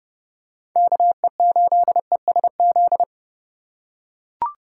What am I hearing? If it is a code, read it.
Morse code, heard as a keyed tone, about 30 wpm, 700 Hz. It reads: KE8ESZ
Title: USA Call Signs - 2x Character Spacing - 30wpm